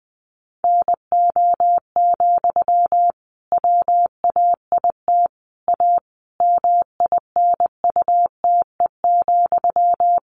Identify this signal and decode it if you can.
Morse code, heard as a keyed tone, about 20 wpm, 700 Hz. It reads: NO, WAIT A MINUTE,
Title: Sentences from Top 1500 Words 20wpm